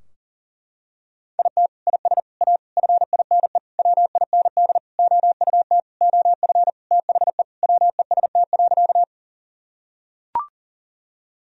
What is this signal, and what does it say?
Morse code, heard as a keyed tone, about 40 wpm, 700 Hz. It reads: IT IS A FINE WIND OUT OF THE WEST.
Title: Sentences from Top 1000 Words - Review Part 2 - 40wpm